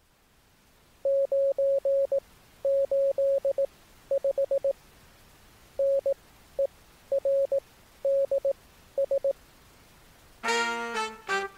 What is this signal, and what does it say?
Morse code, heard as a keyed tone, about 18 wpm, spaced as for 11 wpm, 550 Hz. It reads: 985NERDS